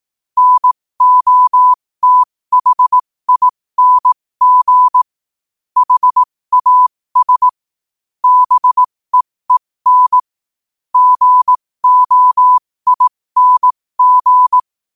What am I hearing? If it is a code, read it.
Morse code, heard as a keyed tone, about 18 wpm, spaced as for 15 wpm, 1000 Hz. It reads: NOTHING HAS BEEN GOING